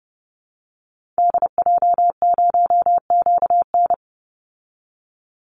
Morse code, heard as a keyed tone, about 30 wpm, 700 Hz. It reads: DJ0QN
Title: Call Signs Repeated 3x - 30wpm